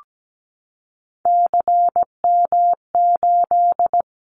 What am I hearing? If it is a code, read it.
Morse code, heard as a keyed tone, about 17 wpm, 700 Hz. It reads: CM8
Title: Callsign Prefixes for Top 100 DXCC - 17wpm